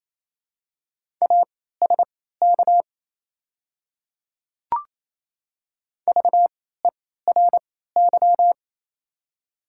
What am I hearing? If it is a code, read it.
Morse code, heard as a keyed tone, about 28 wpm, spaced as for 16 wpm, 700 Hz. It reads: ASK VERY
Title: 3x Character Spacing - Top 300 Words - 28wpm